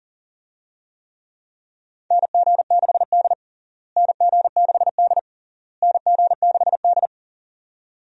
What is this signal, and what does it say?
Morse code, heard as a keyed tone, about 40 wpm, 700 Hz. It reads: NG6D NG6D NG6D
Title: USA Call Signs - 4 Characters - Repeated 3x - 40wpm